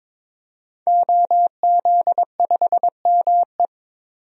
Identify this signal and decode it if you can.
Morse code, heard as a keyed tone, about 22 wpm, 700 Hz. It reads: OZ5ME
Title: DXCC-100 Call Signs - 5 Characters - 22wpm